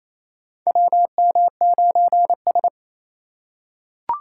Morse code, heard as a keyed tone, about 28 wpm, 700 Hz. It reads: WM9S
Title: USA Call Signs - 4x Character Spacing - 28wpm